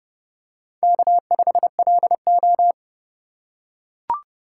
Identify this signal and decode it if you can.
Morse code, heard as a keyed tone, about 30 wpm, 700 Hz. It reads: K5LO
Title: USA Call Signs - 3x Character Spacing - 30wpm